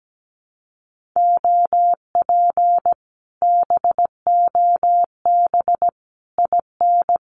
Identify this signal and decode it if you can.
Morse code, heard as a keyed tone, about 17 wpm, 700 Hz. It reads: OP BOB IN
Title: Common QSO Phrases Encore 17wpm